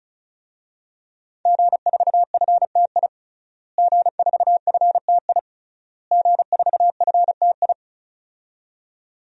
Morse code, heard as a keyed tone, about 34 wpm, 700 Hz. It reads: G4FTI G4FTI G4FTI
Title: DXCC-100 Call Signs - 5 Characters - Repeated 3x - 35wpm